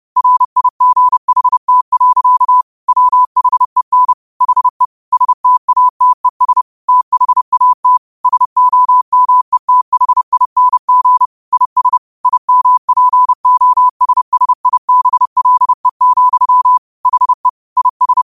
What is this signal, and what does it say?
Morse code, heard as a keyed tone, about 30 wpm, 1000 Hz. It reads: RIGHT. WHEN HE STATES THAT SOMETHING IS IMPOSSIBLE, HE IS
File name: qotdfiles / 30wpm